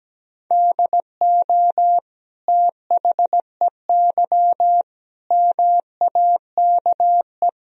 Morse code, heard as a keyed tone, about 17 wpm, 700 Hz. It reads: DO THEY MAKE
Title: Sentences from Top 1000 Words - Review Part 2 - 17wpm